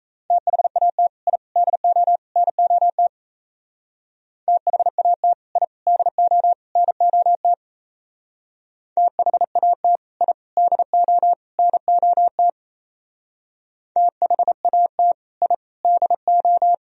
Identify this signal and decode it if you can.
Morse code, heard as a keyed tone, about 35 wpm, 700 Hz. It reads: THAT I DO NOT THAT I DO NOT THAT I DO NOT THAT I DO
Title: Speed-Racing - Sets of 4 Words 28wpm